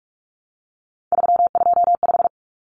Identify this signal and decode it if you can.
Morse code, heard as a keyed tone, about 44 wpm, 700 Hz. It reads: 325
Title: Common Cut Numbers - 3-Digit 45wpm